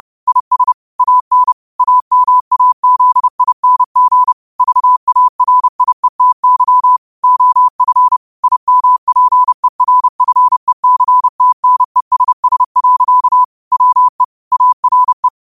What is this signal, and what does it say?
Morse code, heard as a keyed tone, about 30 wpm, 1000 Hz. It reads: IS AN AMAZING VARIETY OF IMPERFECTNESS. WE ARE
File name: qotdfiles / 30wpm